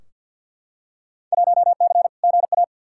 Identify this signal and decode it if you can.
Morse code, heard as a keyed tone, about 50 wpm, 700 Hz. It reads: 1C GA
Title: Field Day Exchanges Encore - 50wpm